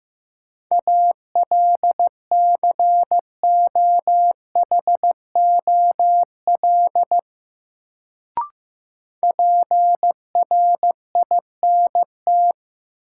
Morse code, heard as a keyed tone, about 15 wpm, 700 Hz. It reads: ALCOHOL PRINT
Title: Sentences from Top 1400 Words - New Word Drill Alt - 15wpm